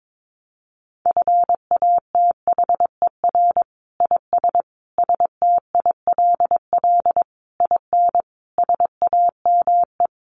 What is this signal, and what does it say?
Morse code, heard as a keyed tone, about 22 wpm, 700 Hz. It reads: FATHER IS STILL IN SAME
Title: Sentences from Top 1000 Words 22wpm